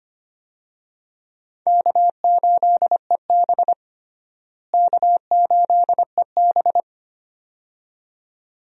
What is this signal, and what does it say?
Morse code, heard as a keyed tone, about 25 wpm, 700 Hz. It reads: K8EB K8EB
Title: USA Call Signs - 4 Characters - Repeated 2x - 25wpm